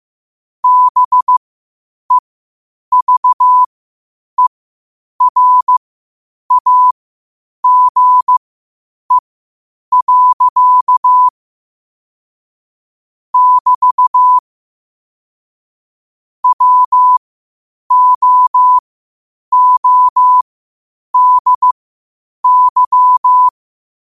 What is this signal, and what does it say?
Morse code, heard as a keyed tone, about 15 wpm, spaced as for 8 wpm, 1000 Hz. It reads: BEVERAGE. <BT> WOODY
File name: qotdfiles / 8wpm